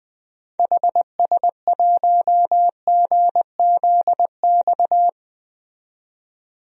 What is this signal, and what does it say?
Morse code, heard as a keyed tone, about 20 wpm, 700 Hz. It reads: HS1GZX
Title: DXCC-100 Call Signs - 6 Characters - 20wpm